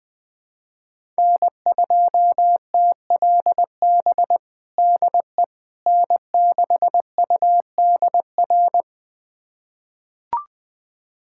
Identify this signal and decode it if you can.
Morse code, heard as a keyed tone, about 21 wpm, 700 Hz. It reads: N2TLB DE N6UDR
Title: USA Callsign DE Callsign - 5 Characters - 20wpm